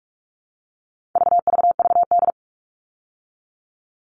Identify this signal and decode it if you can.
Morse code, heard as a keyed tone, about 44 wpm, 700 Hz. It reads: VVVD
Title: All Cut Numbers - 4-Digit 45wpm